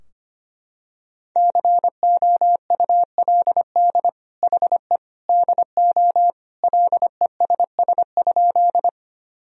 Morse code, heard as a keyed tone, about 25 wpm, 700 Hz. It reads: COULD HE DO LESS?
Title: Sentences from Top 1000 Words - Review Part 2 - 25wpm